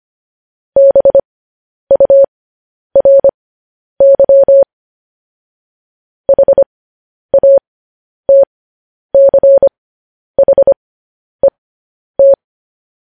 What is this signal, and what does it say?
Morse code, heard as a keyed tone, about 25 wpm, spaced as for 10 wpm, 550 Hz. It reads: BURY HATCHET
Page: news headlines in morse code at 10 wpm